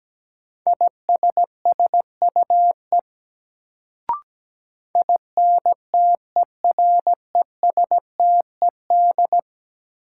Morse code, heard as a keyed tone, about 17 wpm, 700 Hz. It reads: ISSUE INTERESTED
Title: Sentences from Top 900 Words - New Word Drill - 17wpm